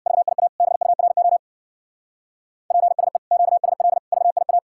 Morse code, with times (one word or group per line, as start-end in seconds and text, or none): 0.1-0.5 VIA
0.6-1.4 BURO
2.7-3.2 PSE
3.3-4.0 QSL
4.1-4.6 VIA